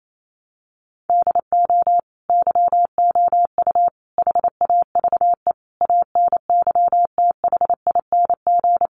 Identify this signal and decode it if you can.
Morse code, heard as a keyed tone, about 28 wpm, 700 Hz. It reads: DO YOU HAVE ANYTHING
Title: Sentences from Top 1000 Words 28wpm